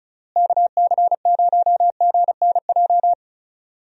Morse code, heard as a keyed tone, about 34 wpm, 700 Hz. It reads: KC0GNJ
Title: USA Call Signs - 6 Characters - Repeated 3x - 35wpm